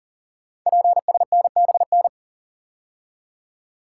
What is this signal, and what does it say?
Morse code, heard as a keyed tone, about 38 wpm, 700 Hz. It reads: PSNBN
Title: Code Groups - 5 Letters - 40wpm